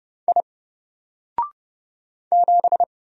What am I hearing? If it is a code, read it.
Morse code, heard as a keyed tone, about 30 wpm, 700 Hz. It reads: I 7
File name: Single-Letter-Number-30wpm